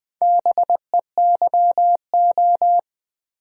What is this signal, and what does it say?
Morse code, heard as a keyed tone, about 20 wpm, 700 Hz. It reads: BEYO
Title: Build A Word - 2 Letters At A Time - 6-Letter Words - 20wpm